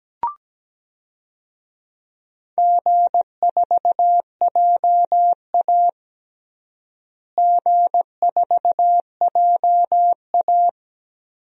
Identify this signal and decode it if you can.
Morse code, heard as a keyed tone, about 17 wpm, 700 Hz. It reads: G4JA G4JA
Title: DXCC-100 Call Signs - 4 Characters - Repeated 3x - 17wpm